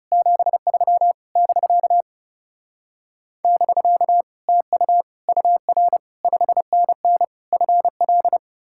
Morse code, heard as a keyed tone, about 40 wpm, 700 Hz. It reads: 73 <BK> <BK> TU UR 5NN FL